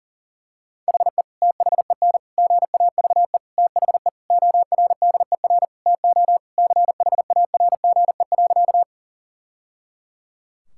Morse code, heard as a keyed tone, about 40 wpm, 700 Hz. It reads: HE THEN GAVE THE ORDER TO CHARGE.
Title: Sentences from Top 1000 Words - Review Part 6 - 40wpm